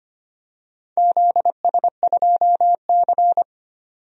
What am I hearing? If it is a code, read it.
Morse code, heard as a keyed tone, about 25 wpm, 700 Hz. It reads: ZS2C
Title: DXCC-100 Call Signs - 4 Characters - 25wpm